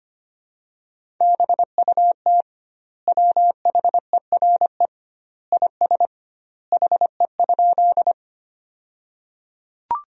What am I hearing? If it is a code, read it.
Morse code, heard as a keyed tone, about 25 wpm, 700 Hz. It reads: BUT WHERE IS HE?
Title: 2x Word-Spacing - Sentences from Top 200 Words - 25wpm